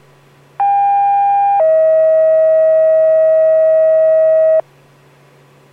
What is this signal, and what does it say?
Two-tone page, heard as a one-second tone, then a three-second tone, about 800 then 640 Hz. It is heard from 0.6 to 4.6 s.